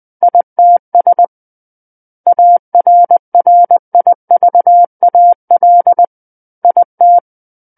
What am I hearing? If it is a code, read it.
Morse code, heard as a keyed tone, about 20 wpm, 700 Hz. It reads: ITS ARRIVAL IT
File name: 20 WPM